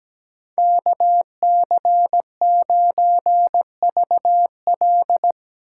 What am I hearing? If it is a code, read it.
Morse code, heard as a keyed tone, about 17 wpm, 700 Hz. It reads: KC9VL
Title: USA Call Signs - 5 Characters - 4x Character Spacing - 17wpm